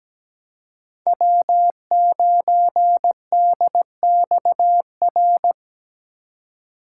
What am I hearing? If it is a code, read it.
Morse code, heard as a keyed tone, about 17 wpm, 700 Hz. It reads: W9DXR
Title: K1USN SST QSO Practice - 17wpm